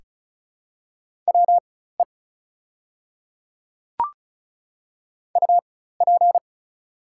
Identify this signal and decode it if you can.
Morse code, heard as a keyed tone, about 36 wpm, spaced as for 16 wpm, 700 Hz. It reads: WE UP
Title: 4x Character Spacing - Top 300 Words - 35wpm